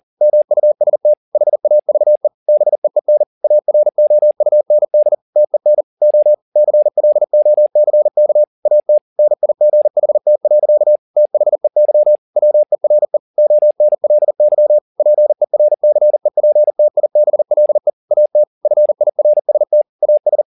Morse code, heard as a keyed tone, about 40 wpm, 600 Hz. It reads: MUST HAVE BEEN AROUND TEN O CLOCK AT NIGHT. THEY WERE ONLY PERCEPTIBLE AT FIRST AS
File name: the old man and the sea 2 40wpm